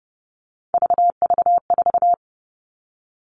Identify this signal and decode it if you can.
Morse code, heard as a keyed tone, about 30 wpm, 700 Hz. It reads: VV4
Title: All Cut Numbers - 3-Digit 30wpm